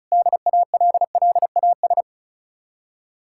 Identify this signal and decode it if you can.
Morse code, heard as a keyed tone, about 46 wpm, 700 Hz. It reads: DALLAS